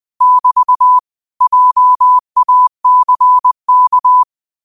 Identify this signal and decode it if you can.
Morse code, heard as a keyed tone, about 20 wpm, 1000 Hz. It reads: <BT> JACK
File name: qotdfiles / 20wpm